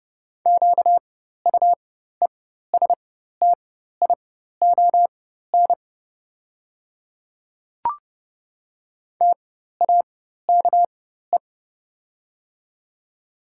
Morse code, heard as a keyed tone, about 30 wpm, spaced as for 14 wpm, 690 Hz. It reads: QUESTION TAKE
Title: 4x Character Spacing - Top 500 Words - 30wpm